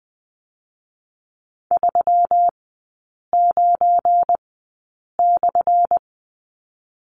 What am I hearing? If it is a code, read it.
Morse code, heard as a keyed tone, about 20 wpm, 700 Hz. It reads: 3 9 /
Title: 184 Lesson 30e - Letter b - Three Characters 20wpm